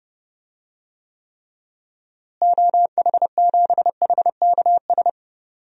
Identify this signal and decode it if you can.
Morse code, heard as a keyed tone, about 31 wpm, 700 Hz. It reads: OH7HKS
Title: DXCC-100 Call Signs - 6 Characters - Repeated 3x - 30wpm